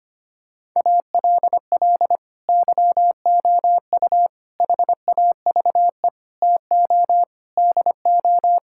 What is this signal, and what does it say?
Morse code, heard as a keyed tone, about 25 wpm, 700 Hz. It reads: ALL YOU HAVE TO DO
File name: practice / 25wpm / Sentences with 9-Word Phrases - 25wpm